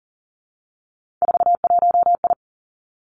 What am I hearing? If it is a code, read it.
Morse code, heard as a keyed tone, about 40 wpm, 700 Hz. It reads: 41I
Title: Three Challenging Letter-Number Combinations 40wpm